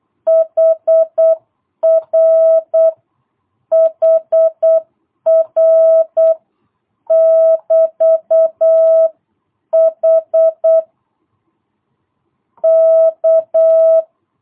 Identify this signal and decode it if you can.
Morse code, heard as a keyed tone, about 8 wpm, 660 Hz. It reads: HR HR <BT> H K
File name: jh8cbh / 2022cw